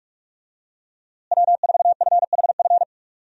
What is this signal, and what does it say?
Morse code, heard as a keyed tone, about 46 wpm, 700 Hz. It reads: W4FHF
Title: USA Call Signs - 5 Characters - Repeated 3x - 45wpm